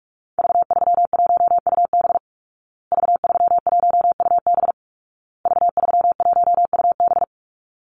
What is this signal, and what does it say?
Morse code, heard as a keyed tone, about 44 wpm, 700 Hz. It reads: V31UB V31UB V31UB
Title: Call Signs Repeated 2x - 45wpm